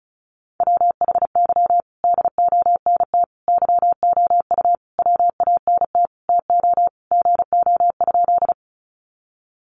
Morse code, heard as a keyed tone, about 34 wpm, 700 Hz. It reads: WHY DONT YOU WANT TO GO?
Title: Sentences from Top 300 Words 35wpm